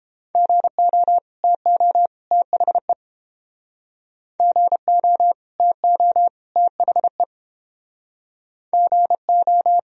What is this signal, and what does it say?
Morse code, heard as a keyed tone, about 31 wpm, 700 Hz. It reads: GO TO THE GO TO THE GO